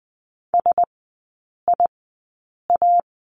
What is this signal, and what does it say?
Morse code, heard as a keyed tone, about 20 wpm, spaced as for 8 wpm, 700 Hz. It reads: SIA
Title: 041 Lesson 8e - Number 1 - Three Characters 20wpm